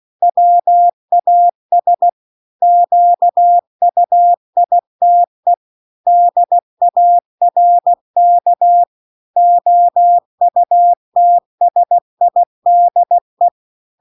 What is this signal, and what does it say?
Morse code, heard as a keyed tone, about 16 wpm, 700 Hz. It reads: WAS QUITE DARK OUTSIDE